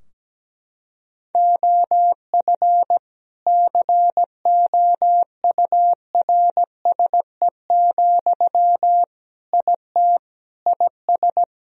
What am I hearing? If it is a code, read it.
Morse code, heard as a keyed tone, about 17 wpm, 700 Hz. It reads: OF COURSE, IT IS
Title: Sentences from Top 1000 Words - Review Part 2 - 17wpm